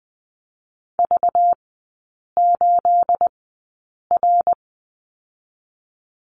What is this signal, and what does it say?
Morse code, heard as a keyed tone, about 20 wpm, 700 Hz. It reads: V 8 R
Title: 212 Lesson 34e - Number 8 - Three Characters 20wpm